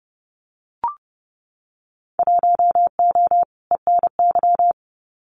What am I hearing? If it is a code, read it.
Morse code, heard as a keyed tone, about 30 wpm, 700 Hz. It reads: 1O ENY
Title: Winter Field Day Exchanges 30wpm